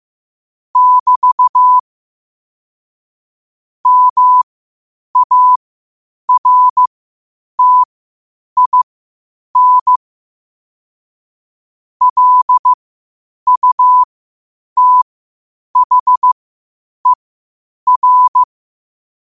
Morse code, heard as a keyed tone, about 15 wpm, spaced as for 8 wpm, 1000 Hz. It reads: <BT> MARTIN LUTHER
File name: qotdfiles / 8wpm